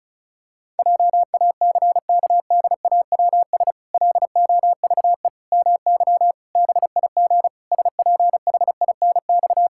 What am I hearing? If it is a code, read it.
Morse code, heard as a keyed tone, about 34 wpm, 700 Hz. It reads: JACKDAWS LOVE MY BIG SPHINX
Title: Pangram Challenge v2 - 35wpm